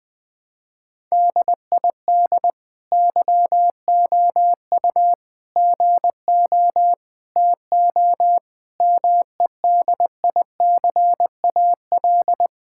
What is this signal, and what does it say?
Morse code, heard as a keyed tone, about 21 wpm, 700 Hz. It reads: DID YOU GO TO MEDICAL
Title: Sentences from Top 1000 Words - Review Part 5 - 20wpm